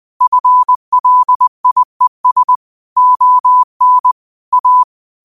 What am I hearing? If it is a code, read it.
Morse code, heard as a keyed tone, about 20 wpm, 1000 Hz. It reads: FLIES ON A